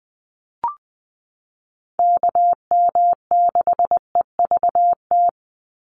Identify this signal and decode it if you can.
Morse code, heard as a keyed tone, about 20 wpm, 700 Hz. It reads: KM6EVT
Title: USA Call Signs 20wpm